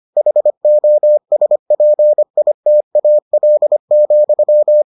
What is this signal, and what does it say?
Morse code, heard as a keyed tone, about 25 wpm, 600 Hz. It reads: HOSPITAL,